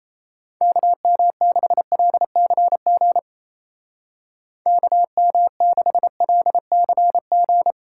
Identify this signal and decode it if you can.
Morse code, heard as a keyed tone, about 34 wpm, 700 Hz. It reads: KM6LCG KM6LCG
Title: Speed Racing - USA 2x3 Call Signs 28wpm